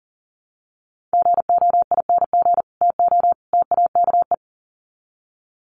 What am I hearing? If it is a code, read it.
Morse code, heard as a keyed tone, about 40 wpm, 700 Hz. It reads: GOING TO TAKE
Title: Sets of 3 Words 40wpm